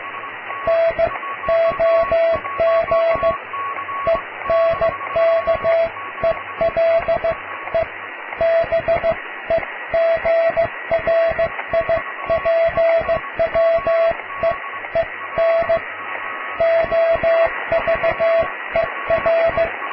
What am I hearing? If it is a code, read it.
Morse code, heard as a keyed tone, about 15 wpm, spaced as for 13 wpm, 660 Hz, with radio noise. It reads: NOG ENKELE BEGRIPWEEN OVER
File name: cw / tekst(a)+QRM